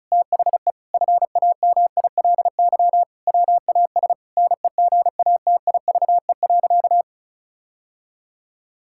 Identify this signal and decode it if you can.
Morse code, heard as a keyed tone, about 34 wpm, 700 Hz. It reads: THE FAMILY WAS NEGATIVE.
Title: Sentences from Top 1000 Words - Review Part 6 - 35wpm